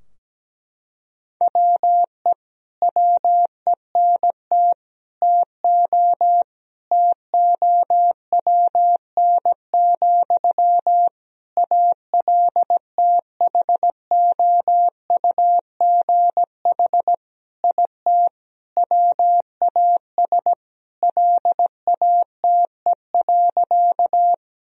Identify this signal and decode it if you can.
Morse code, heard as a keyed tone, about 17 wpm, 700 Hz. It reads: WE WENT TO TOWN, ALTHOUGH IT WAS LATE.
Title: Sentences from Top 1000 Words - Review Part 3 - 17wpm